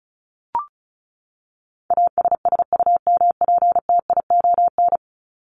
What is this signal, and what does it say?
Morse code, heard as a keyed tone, about 34 wpm, 700 Hz. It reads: ASSUMPTION
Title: Top 1751-2000 Words 35wpm